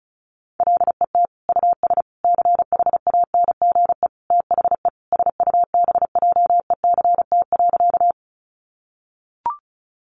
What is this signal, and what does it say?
Morse code, heard as a keyed tone, about 34 wpm, 700 Hz. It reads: LET US CHANGE THE SUBJECT.
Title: Sentences from Top 800 Words 35wpm